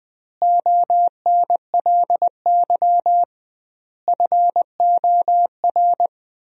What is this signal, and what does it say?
Morse code, heard as a keyed tone, about 21 wpm, 700 Hz. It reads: ONLY FOR
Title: 2x Word-Spacing - Sentences from Top 500 Words - 20wpm